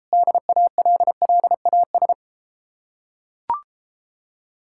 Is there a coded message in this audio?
Yes: Morse code from 0.1 to 2.1 s, heard as a keyed tone, about 33 wpm, 700 Hz. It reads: DALLAS